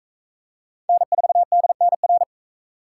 Morse code, heard as a keyed tone, about 41 wpm, 700 Hz. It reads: N4DNR